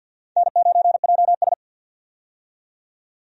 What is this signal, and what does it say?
Morse code, heard as a keyed tone, about 52 wpm, 700 Hz. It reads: N9JS
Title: USA Call Signs - 4 Characters - 50wpm